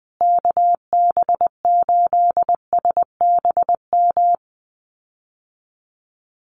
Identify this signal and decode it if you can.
Morse code, heard as a keyed tone, about 20 wpm, 700 Hz. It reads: KB8SBM
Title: USA Call Signs Repeated 2x 20wpm